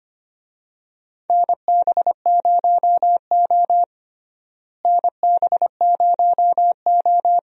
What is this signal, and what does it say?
Morse code, heard as a keyed tone, about 25 wpm, 700 Hz. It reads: NB0O NB0O
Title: USA Call Signs - 4 Characters - Repeated 2x - 25wpm